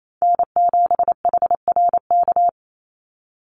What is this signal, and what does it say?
Morse code, heard as a keyed tone, about 28 wpm, 700 Hz. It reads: N7HRK
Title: Call Signs - 28wpm